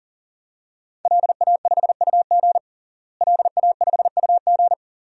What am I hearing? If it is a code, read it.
Morse code, heard as a keyed tone, about 37 wpm, 700 Hz. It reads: LA5UG LA5UG
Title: DXCC-100 Call Signs - 5 Characters - Repeated 2x - 40wpm